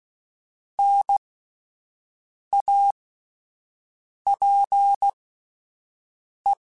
Morse code, heard as a keyed tone, about 16 wpm, spaced as for 5 wpm, 800 Hz. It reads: NAPE